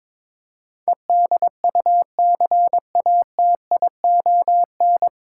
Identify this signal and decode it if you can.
Morse code, heard as a keyed tone, about 22 wpm, 700 Hz. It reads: EDUCATION
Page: Morse Code Ninja practice 22wpm